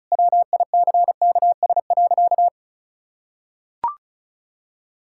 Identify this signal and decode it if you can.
Morse code, heard as a keyed tone, about 34 wpm, 700 Hz. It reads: WICKS.
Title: Pangram Challenge v2 - 35wpm